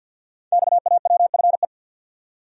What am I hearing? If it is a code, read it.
Morse code, heard as a keyed tone, about 50 wpm, 700 Hz. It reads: XAWFE
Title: Code Groups - 5 Letters - 50wpm